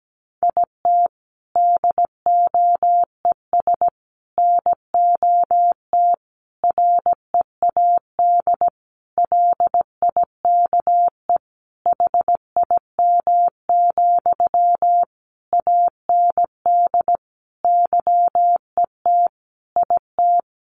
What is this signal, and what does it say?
Morse code, heard as a keyed tone, about 17 wpm, 700 Hz. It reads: IT DOES NOT READ LIKE HIM, AND YET IT
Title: Sentences from Top 800 Words 17wpm